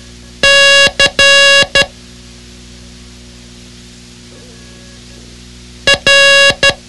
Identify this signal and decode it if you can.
Morse code, heard as a keyed tone, about 12 wpm, 2800 Hz. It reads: C R
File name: ham radio / CodeQuick / Code Quick 06